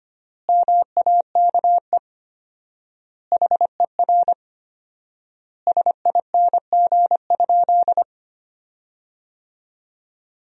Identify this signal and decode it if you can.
Morse code, heard as a keyed tone, about 25 wpm, 700 Hz. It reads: MAKE HER SING?
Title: 4x Word-Spacing - Sentences from Top 400 Words - 25wpm